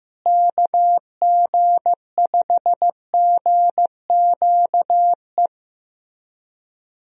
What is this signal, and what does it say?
Morse code, heard as a keyed tone, about 15 wpm, 700 Hz. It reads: KG5GQE
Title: USA Call Signs - 6 Characters - Repeated 3x - 15wpm